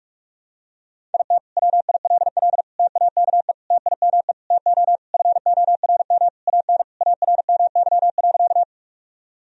Morse code, heard as a keyed tone, about 42 wpm, 700 Hz. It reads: IT WILL TAKE TIME TO FORM AN ARMY.
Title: Sentences from Top 1000 Words - Review Part 1 - 45wpm